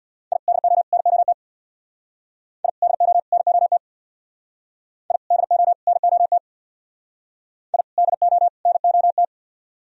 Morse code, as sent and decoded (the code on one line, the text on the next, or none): .. / -.. --- / -. --- - / .. / -.. --- / -. --- - / .. / -.. --- / -. --- - / .. / -.. --- / -. --- -
I DO NOT I DO NOT I DO NOT I DO NOT